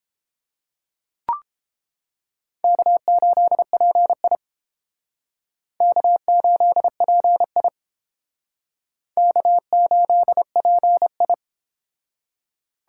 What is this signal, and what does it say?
Morse code, heard as a keyed tone, about 30 wpm, 700 Hz. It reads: K8PI K8PI K8PI